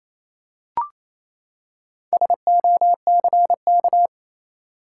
Morse code, heard as a keyed tone, about 28 wpm, 700 Hz. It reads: SOCK
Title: Four Letter Words - 28wpm